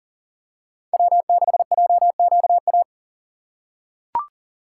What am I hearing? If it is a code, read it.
Morse code, heard as a keyed tone, about 40 wpm, 700 Hz. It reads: W6JQA
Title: USA Call Signs - 5 Characters - 4x Character Spacing - 40wpm